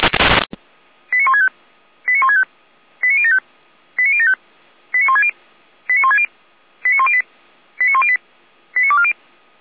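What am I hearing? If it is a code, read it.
Five-tone selective call (ZVEI), heard as quick runs of five tones, tones of about 70 ms. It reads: 89165 89165 89075 89075 89160 89160 89108 89108 89240
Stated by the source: Received on SDR as a ZVEI five-tone selective call, code 89150.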